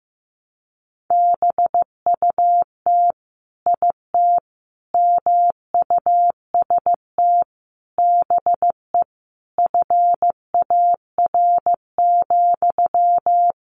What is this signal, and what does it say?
Morse code, heard as a keyed tone, about 15 wpm, 700 Hz. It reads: BUT IT MUST BE FAR,
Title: Sentences from Top 600 Words 15wpm